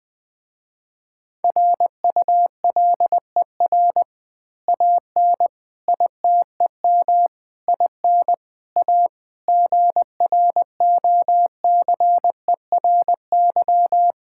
Morse code, heard as a keyed tone, about 20 wpm, 700 Hz. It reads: RULER AN ITEM IN A GROCERY
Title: Easy Words with Hints - 20wpm